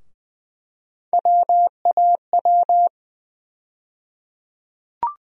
Morse code, heard as a keyed tone, about 20 wpm, 700 Hz. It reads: WAW
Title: USA Call Sign Postfixes - 3 Letters 20wpm